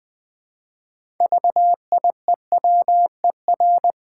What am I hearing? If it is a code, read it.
Morse code, heard as a keyed tone, about 20 wpm, 700 Hz. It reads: VIEWER
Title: Build A Word - 3 Letters At A Time - 6-Letter Words - 20wpm